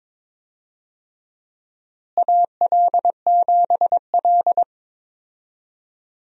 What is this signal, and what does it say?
Morse code, heard as a keyed tone, about 22 wpm, 700 Hz. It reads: AL7L
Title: USA Call Signs - 4 Characters - Repeated 3x - 22wpm